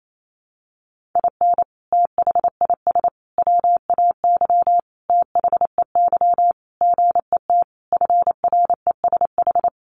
Morse code, heard as a keyed tone, about 28 wpm, 700 Hz. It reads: IN THIS WAY THEY GET FRESH
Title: Sentences from Top 1200 Words 28wpm